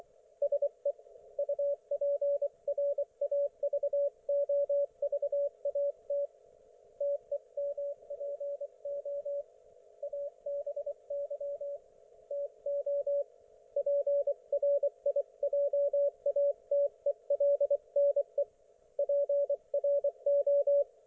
Morse code, heard as a keyed tone, about 24 wpm, 570 Hz. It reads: SE UPRAVOVAT TEMPO ABY TO PRIJATELNE PRO
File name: Archiv / CUC224 / ok1fhi cul224